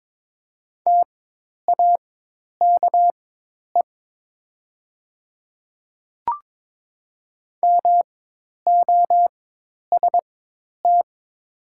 Morse code, heard as a keyed tone, about 22 wpm, spaced as for 10 wpm, 690 Hz. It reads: TAKE MOST
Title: 4x Character Spacing - Top 200 Words - 22wpm